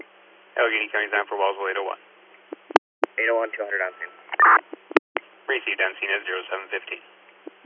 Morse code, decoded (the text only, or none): none